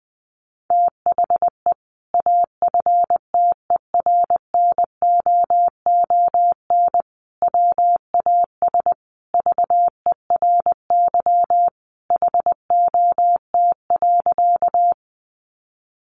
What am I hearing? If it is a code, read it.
Morse code, heard as a keyed tone, about 19 wpm, 700 Hz. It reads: THE AFTERNOON WAS VERY HOT.
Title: Sentences from Top 1500 Words 20wpm